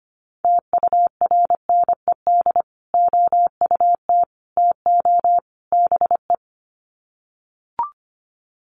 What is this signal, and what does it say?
Morse code, heard as a keyed tone, about 25 wpm, 700 Hz. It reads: TURNED OUT TO BE
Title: Sets of 4 Words 25wpm